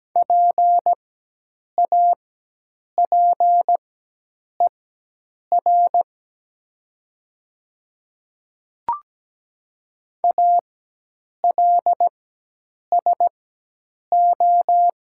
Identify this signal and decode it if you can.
Morse code, heard as a keyed tone, about 17 wpm, spaced as for 8 wpm, 700 Hz. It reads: PAPER ALSO
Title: 4x Character Spacing - Top 300 Words - 17wpm